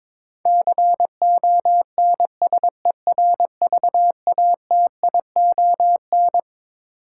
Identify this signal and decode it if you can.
Morse code, heard as a keyed tone, about 21 wpm, 700 Hz. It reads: CONSERVATION
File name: Twelve Letter Words - 22wpm